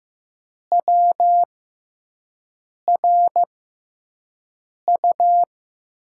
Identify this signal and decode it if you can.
Morse code, heard as a keyed tone, about 15 wpm, spaced as for 5 wpm, 700 Hz. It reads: WRU
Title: USA Call Signs - 6 Characters - 6x Character Spacing - 15wpm